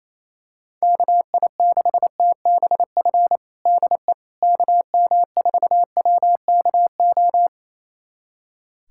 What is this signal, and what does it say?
Morse code, heard as a keyed tone, about 28 wpm, 700 Hz. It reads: KI6TBF DE KM4WKO
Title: USA Callsign DE Callsign 28wpm